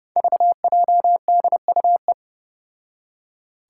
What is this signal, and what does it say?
Morse code, heard as a keyed tone, about 31 wpm, 700 Hz. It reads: VJDUE
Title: Five Challenging Letter-Number Combinations 30wpm